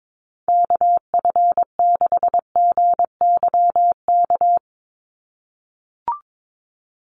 Morse code, heard as a keyed tone, about 22 wpm, 700 Hz. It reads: KF6GYK
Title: USA Call Signs 22wpm